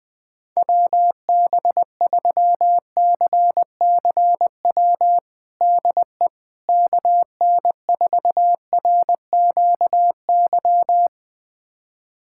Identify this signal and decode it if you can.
Morse code, heard as a keyed tone, about 20 wpm, 700 Hz. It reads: WB3CCW DE KN4RQY
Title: USA Callsign DE Callsign - 6 Characters - 20wpm